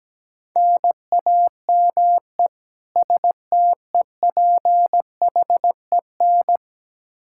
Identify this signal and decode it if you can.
Morse code, heard as a keyed tone, about 17 wpm, 700 Hz. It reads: NAME STEPHEN
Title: Practice QSO - Exchange 1 - Beginning - 17wpm